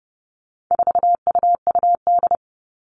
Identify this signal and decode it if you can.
Morse code, heard as a keyed tone, about 30 wpm, 700 Hz. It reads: 4UUD
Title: All Cut Numbers - 4-Digit 30wpm